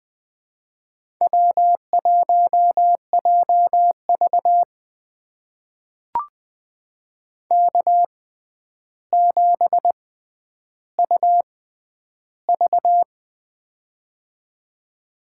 Morse code, heard as a keyed tone, about 20 wpm, spaced as for 10 wpm, 700 Hz. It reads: W1JV K7UV
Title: USA Call Signs - 4 Characters - 6x Character Spacing - 20wpm